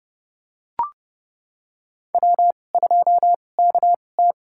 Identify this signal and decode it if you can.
Morse code, heard as a keyed tone, about 30 wpm, spaced as for 22 wpm, 700 Hz. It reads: W2KT
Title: USA Call Signs - 4 Characters - 2x Character Spacing - 30wpm